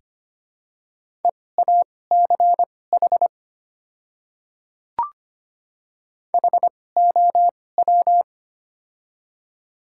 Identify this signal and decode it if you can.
Morse code, heard as a keyed tone, about 25 wpm, spaced as for 18 wpm, 690 Hz. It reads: EACH HOW
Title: 2x Character Spacing - Top 200 Words - 25wpm